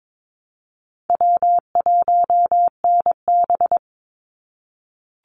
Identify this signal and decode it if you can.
Morse code, heard as a keyed tone, about 22 wpm, 700 Hz. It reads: W1NB
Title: USA Call Signs 22wpm